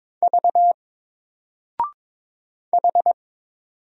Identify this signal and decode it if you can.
Morse code, heard as a keyed tone, about 22 wpm, 700 Hz. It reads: V H
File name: Single-Letter-Number-22wpm